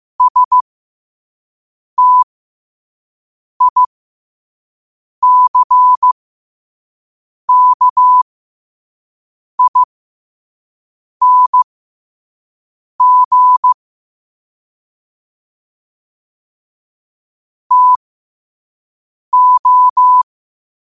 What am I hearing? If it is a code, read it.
Morse code, heard as a keyed tone, about 15 wpm, spaced as for 5 wpm, 1000 Hz. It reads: STICKING TO
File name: qotdfiles / 5wpm